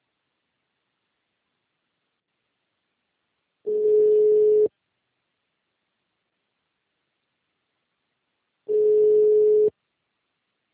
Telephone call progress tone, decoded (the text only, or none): ringback tone